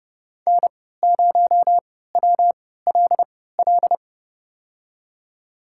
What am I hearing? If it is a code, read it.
Morse code, heard as a keyed tone, about 32 wpm, spaced as for 17 wpm, 700 Hz. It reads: N0WLL
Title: USA Call Signs - 3x Character Spacing - 30wpm